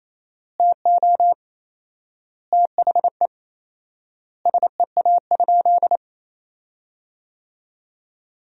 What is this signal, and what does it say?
Morse code, heard as a keyed tone, about 28 wpm, 700 Hz. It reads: TO THE SEA?
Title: 4x Word-Spacing - Sentences from Top 300 Words - 28wpm